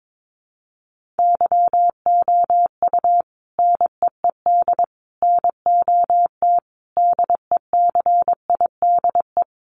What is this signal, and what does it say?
Morse code, heard as a keyed tone, about 22 wpm, 700 Hz. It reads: YOU NEED NOT DECIDE
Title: Sentences from Top 1500 Words 22wpm